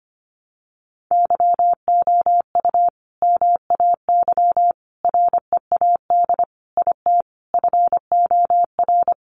Morse code, heard as a keyed tone, about 25 wpm, 700 Hz. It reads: YOU MAY READ IT FOR
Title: Sentences from Top 600 Words 25wpm